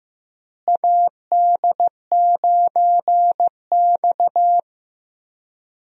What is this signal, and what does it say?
Morse code, heard as a keyed tone, about 16 wpm, 700 Hz. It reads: AD9X
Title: USA Call Signs - 4 Characters - Repeated 2x - 15wpm